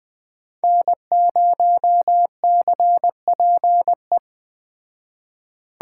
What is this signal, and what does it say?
Morse code, heard as a keyed tone, about 20 wpm, 700 Hz. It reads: N0CPE